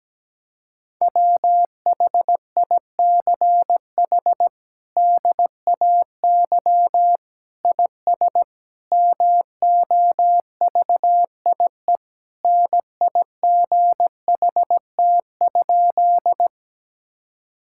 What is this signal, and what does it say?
Morse code, heard as a keyed tone, about 17 wpm, 700 Hz. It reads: WHICH DAY IS MOVIE NIGHT?
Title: Sentences from Top 1000 Words - Review Part 5 - 17wpm